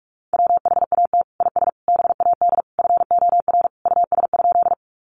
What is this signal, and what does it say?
Morse code, heard as a keyed tone, about 44 wpm, 700 Hz. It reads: WHAT IS BAD FOR US?
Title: Sentences from Top 700 Words 45wpm